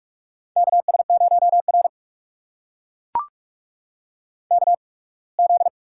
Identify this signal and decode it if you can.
Morse code, heard as a keyed tone, about 43 wpm, spaced as for 26 wpm, 700 Hz. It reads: KS0R KZ
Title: USA Call Signs - 4 Characters - 8x Character Spacing - 45wpm